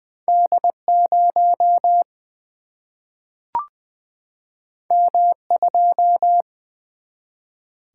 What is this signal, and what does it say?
Morse code, heard as a keyed tone, about 21 wpm, 700 Hz. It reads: D 0 M 2
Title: Single-Letter Single-Number 20wpm